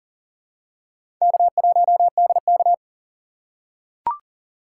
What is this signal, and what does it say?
Morse code, heard as a keyed tone, about 40 wpm, 700 Hz. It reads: K1DK
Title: USA Call Signs - 4 Characters - 2x Character Spacing - 40wpm